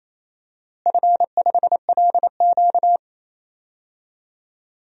Morse code, heard as a keyed tone, about 28 wpm, 700 Hz. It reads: F5LQ
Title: DXCC-100 Call Signs - 4 Characters - Repeated 3x - 28wpm